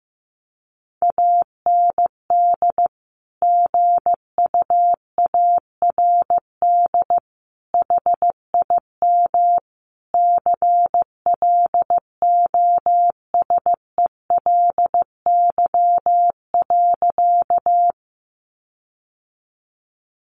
Morse code, heard as a keyed tone, about 15 wpm, 700 Hz. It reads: AND GUARD HIM CLOSELY.
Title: Sentences from Top 1400 Words 15wpm